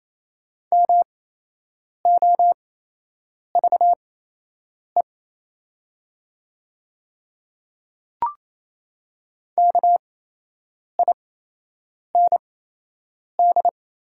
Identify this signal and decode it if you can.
Morse code, heard as a keyed tone, about 28 wpm, spaced as for 8 wpm, 700 Hz. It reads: MOVE KIND
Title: 8x Character Spacing - Top 300 Words - 28wpm